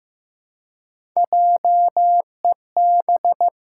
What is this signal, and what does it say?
Morse code, heard as a keyed tone, about 14 wpm, 700 Hz. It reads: JEB
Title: Pangram Challenge v2 - 15wpm